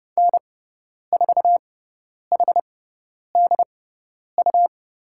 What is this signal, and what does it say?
Morse code, heard as a keyed tone, about 30 wpm, spaced as for 10 wpm, 700 Hz. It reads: N4HDU
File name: Lesson 17g - Letter u - Call Signs 30wpm